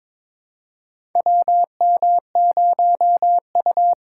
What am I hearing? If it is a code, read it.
Morse code, heard as a keyed tone, about 22 wpm, 700 Hz. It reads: WM0U
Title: USA Call Signs - 2x Character Spacing - 22wpm